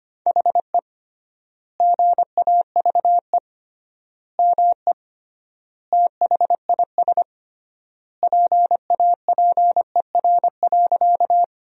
Morse code, heard as a keyed tone, about 25 wpm, 700 Hz. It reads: HE GAVE ME THIS PAPER.
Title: 3x Word-Spacing - Sentences from Top 500 Words - 25wpm